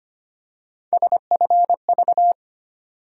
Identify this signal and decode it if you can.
Morse code, heard as a keyed tone, about 25 wpm, 700 Hz. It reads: SFV
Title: USA Call Sign Postfixes - 3 Letters 25wpm